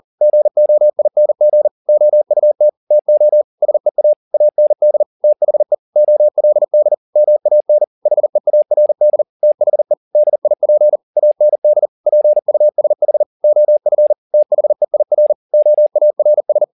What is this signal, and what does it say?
Morse code, heard as a keyed tone, about 40 wpm, 600 Hz. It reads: GOING OUT TO SEA AND THE OLD MAN HEARD THE DIP AND PUSH OF THEIR OARS